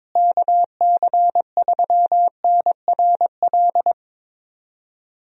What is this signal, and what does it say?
Morse code, heard as a keyed tone, about 21 wpm, 700 Hz. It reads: KC3NRL
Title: USA Call Signs - 6 Characters - Repeated 3x - 22wpm